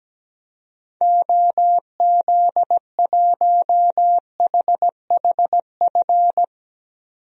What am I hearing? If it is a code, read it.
Morse code, heard as a keyed tone, about 17 wpm, 700 Hz. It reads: OZ1HHF
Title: DXCC-100 Call Signs - 6 Characters - 17wpm